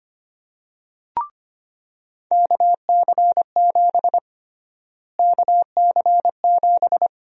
Morse code, heard as a keyed tone, about 25 wpm, 700 Hz. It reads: KC7 KC7
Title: USA Call Sign Prefixes - Repeated 2x - 25wpm